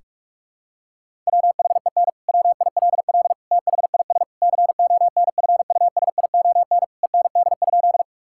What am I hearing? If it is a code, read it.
Morse code, heard as a keyed tone, about 46 wpm, 700 Hz. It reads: WHEN WILL THIS CONFUSION END?
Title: Sentences from Top 1600 Words 45wpm